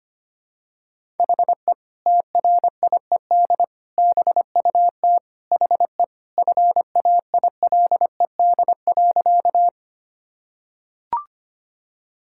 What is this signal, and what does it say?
Morse code, heard as a keyed tone, about 25 wpm, 700 Hz. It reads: HE TRIED BUT HE FAILED.
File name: practice / 25wpm / Sentences with 2-Word Phrases - 25wpm